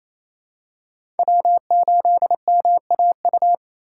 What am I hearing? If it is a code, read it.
Morse code, heard as a keyed tone, about 28 wpm, 700 Hz. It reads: W8MAU
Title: USA Call Signs - 5 Characters - 6x Character Spacing - 28wpm